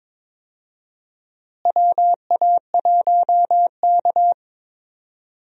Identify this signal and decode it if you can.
Morse code, heard as a keyed tone, about 22 wpm, 700 Hz. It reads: WA1K
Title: USA Call Signs - 4 Characters - Repeated 3x - 22wpm